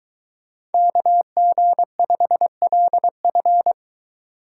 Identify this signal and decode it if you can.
Morse code, heard as a keyed tone, about 23 wpm, 700 Hz. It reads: KG5LF